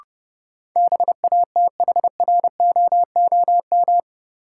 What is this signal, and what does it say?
Morse code, heard as a keyed tone, about 30 wpm, 700 Hz. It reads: BATHROOM
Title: Sets of 5 Words Encore - New Word Drill - 30wpm